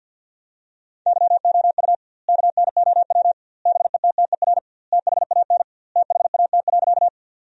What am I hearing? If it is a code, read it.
Morse code, heard as a keyed tone, about 50 wpm, 700 Hz. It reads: YOU KNOW BETTER THAN THAT.
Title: 2x Word-Spacing - Sentences from Top 400 Words - 50wpm